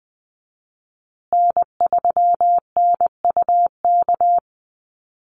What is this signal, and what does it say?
Morse code, heard as a keyed tone, about 20 wpm, 700 Hz. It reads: N3NUK
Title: USA Call Signs Repeated 2x 20wpm